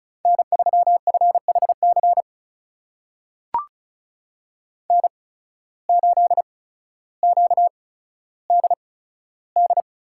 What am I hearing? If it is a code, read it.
Morse code, heard as a keyed tone, about 36 wpm, spaced as for 15 wpm, 700 Hz. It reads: N3FHC N8QDD
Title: USA Call Signs - 8x Character Spacing - 35wpm